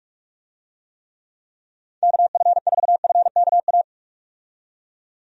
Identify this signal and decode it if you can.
Morse code, heard as a keyed tone, about 46 wpm, 700 Hz. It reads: KF4FKA
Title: USA Call Signs - 6 Characters - 45wpm